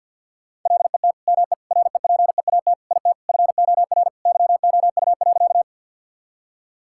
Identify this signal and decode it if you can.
Morse code, heard as a keyed tone, about 50 wpm, 700 Hz. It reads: LET ME REPEAT IT FOR YOU.
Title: Sentences from Top 1600 Words 50wpm